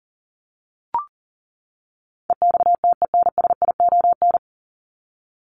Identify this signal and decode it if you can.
Morse code, heard as a keyed tone, about 40 wpm, 700 Hz. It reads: EXTENSION
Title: Top 1501-1750 Words 40wpm